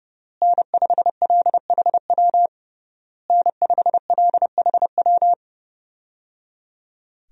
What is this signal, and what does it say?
Morse code, heard as a keyed tone, about 28 wpm, 700 Hz. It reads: N5LHW N5LHW
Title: USA Call Signs - 5 Characters - Repeated 3x - 30wpm